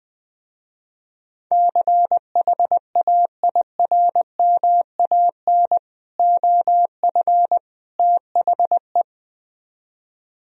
Morse code, heard as a keyed tone, about 20 wpm, 700 Hz. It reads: CHAIRMAN OF THE
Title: Speed-Racing - Sets of 3 Words 20wpm